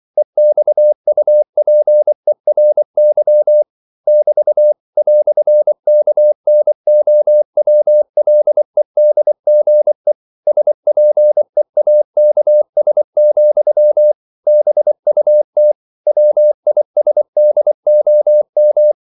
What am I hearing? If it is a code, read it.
Morse code, heard as a keyed tone, about 24 wpm, 600 Hz. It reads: EXUPERY <BT> "KNOWLEDGE SPEAKS, BUT WISDOM
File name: CW / Practice / EnglishQuotations0224